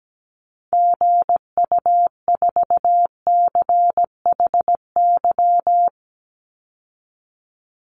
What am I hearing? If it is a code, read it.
Morse code, heard as a keyed tone, about 17 wpm, 700 Hz. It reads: GU4CHY
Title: Call Signs Repeated 3x - 17wpm